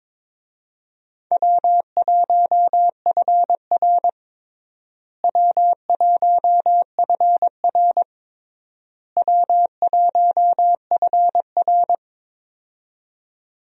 Morse code, heard as a keyed tone, about 22 wpm, 700 Hz. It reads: W1FR W1FR W1FR
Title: USA Call Signs - 4 Characters - Repeated 3x - 22wpm